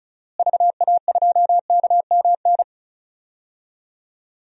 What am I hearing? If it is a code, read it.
Morse code, heard as a keyed tone, about 36 wpm, 700 Hz. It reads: VA2KMN
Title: DXCC-100 Call Signs - 6 Characters - Repeated 3x - 35wpm